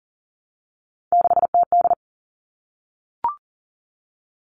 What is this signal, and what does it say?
Morse code, heard as a keyed tone, about 40 wpm, 700 Hz. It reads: 6TD
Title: All Cut Numbers - 3-Digit 40wpm